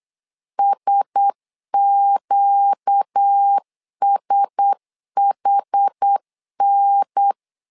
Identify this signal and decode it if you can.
Morse code, heard as a keyed tone, about 8 wpm, 800 Hz. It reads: SQSHN
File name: LEZIONI / 4 Q1S 8wpm 2710